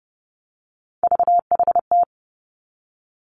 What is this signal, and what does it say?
Morse code, heard as a keyed tone, about 30 wpm, 700 Hz. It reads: VHT
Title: Three Challenging Letter-Number Combinations 30wpm